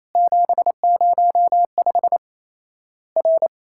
Morse code, heard as a keyed tone, about 28 wpm, 700 Hz. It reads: 705 R
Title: Worked All Europe DX Contest - QTC Exchange - 28wpm